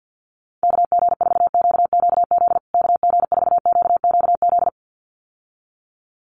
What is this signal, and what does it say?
Morse code, heard as a keyed tone, about 50 wpm, 700 Hz. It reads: KG4QQZ KG4QQZ
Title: USA Call Signs Repeated 2x 50wpm